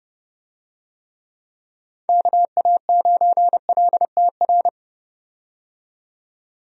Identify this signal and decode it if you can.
Morse code, heard as a keyed tone, about 30 wpm, 700 Hz. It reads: KA9LTR
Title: USA Call Signs - 6 Characters - 30wpm